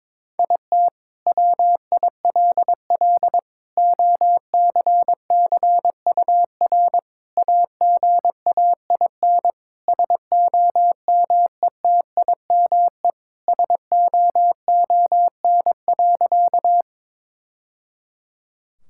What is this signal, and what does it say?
Morse code, heard as a keyed tone, about 22 wpm, 700 Hz. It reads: IT WILL OCCUR AGAIN SOMETIME SOON.
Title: Sentences from Top 1000 Words - Review Part 6 - 22wpm